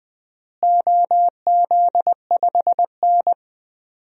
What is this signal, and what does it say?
Morse code, heard as a keyed tone, about 21 wpm, 700 Hz. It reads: OZ5N
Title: DXCC-100 Call Signs - 4 Characters - Repeated 3x - 20wpm